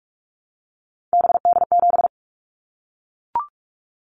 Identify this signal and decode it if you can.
Morse code, heard as a keyed tone, about 44 wpm, 700 Hz. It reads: BD7
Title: Callsign Prefixes for Top 100 DXCC - 45wpm